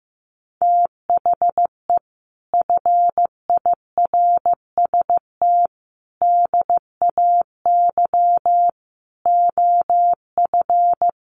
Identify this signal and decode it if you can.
Morse code, heard as a keyed tone, about 15 wpm, 700 Hz. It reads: THE FIRST DAY OF
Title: Sets of 5 Words Encore2 15wpm